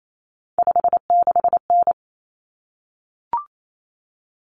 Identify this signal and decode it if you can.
Morse code, heard as a keyed tone, about 28 wpm, 700 Hz. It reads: 56N
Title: Common Cut Numbers - 3-Digit 28wpm